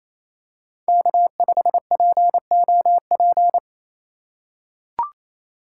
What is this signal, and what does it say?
Morse code, heard as a keyed tone, about 28 wpm, 700 Hz. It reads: K5POP
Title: USA Call Signs - 5 Characters - 4x Character Spacing - 28wpm